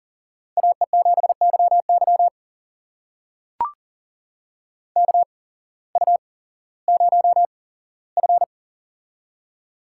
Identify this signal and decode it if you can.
Morse code, heard as a keyed tone, about 40 wpm, spaced as for 19 wpm, 700 Hz. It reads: AE7YY KU0F
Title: USA Call Signs - 8x Character Spacing - 40wpm